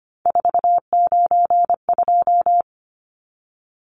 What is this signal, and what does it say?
Morse code, heard as a keyed tone, about 25 wpm, 690 Hz. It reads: 492
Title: Numbers - 3-Digit 25wpm